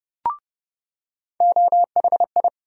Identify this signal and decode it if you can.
Morse code, heard as a keyed tone, about 30 wpm, 700 Hz. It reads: OHI
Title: USA Call Sign Postfixes - 3 Letters 30wpm